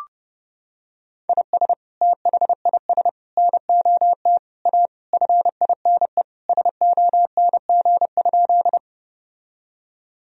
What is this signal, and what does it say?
Morse code, heard as a keyed tone, about 30 wpm, 700 Hz. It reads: IS THIS NOT A FINE SONG?
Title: Sentences from Top 1000 Words - Review Part 2 - 30wpm